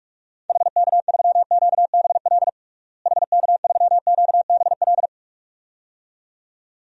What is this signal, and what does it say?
Morse code, heard as a keyed tone, about 46 wpm, 700 Hz. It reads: HK3QBL HK3QBL
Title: DXCC-100 Call Signs - 6 Characters - Repeated 2x - 45wpm